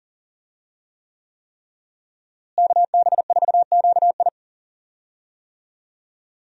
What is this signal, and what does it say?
Morse code, heard as a keyed tone, about 38 wpm, 700 Hz. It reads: KB4QI
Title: USA Call Signs - 5 Characters - Repeated 3x - 40wpm